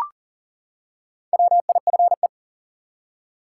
Morse code, heard as a keyed tone, about 38 wpm, 700 Hz. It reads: WIFE
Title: Four Letter Words - 40wpm